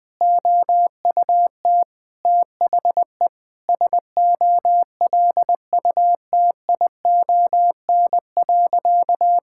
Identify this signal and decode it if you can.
Morse code, heard as a keyed tone, about 20 wpm, 700 Hz. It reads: OUT THE SOLUTION.
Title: Sentences with 8-Word Phrases - Spoken Beginning - 20wpm